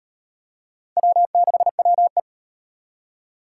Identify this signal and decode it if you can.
Morse code, heard as a keyed tone, about 33 wpm, 700 Hz. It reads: W6WE